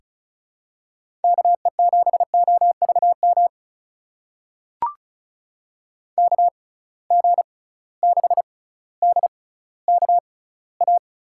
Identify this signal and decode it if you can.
Morse code, heard as a keyed tone, about 36 wpm, spaced as for 18 wpm, 690 Hz. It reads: KE7OVM KG6DKA
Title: USA Call Signs - 6 Characters - 6x Character Spacing - 35wpm